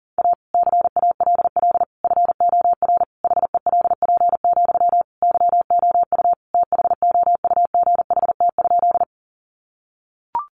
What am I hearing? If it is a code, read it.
Morse code, heard as a keyed tone, about 40 wpm, 700 Hz. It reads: A CALL FOR HELP, YOU THOUGHT?
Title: Sentences from Top 300 Words 40wpm